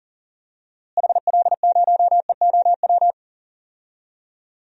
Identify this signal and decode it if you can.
Morse code, heard as a keyed tone, about 37 wpm, 700 Hz. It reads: HL0EOW
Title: DXCC-100 Call Signs - 6 Characters - Repeated 2x - 40wpm